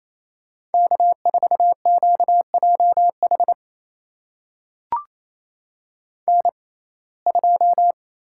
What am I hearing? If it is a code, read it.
Morse code, heard as a keyed tone, about 28 wpm, spaced as for 20 wpm, 700 Hz. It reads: K4QJH N2
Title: USA Call Signs - 5 Characters - 6x Character Spacing - 28wpm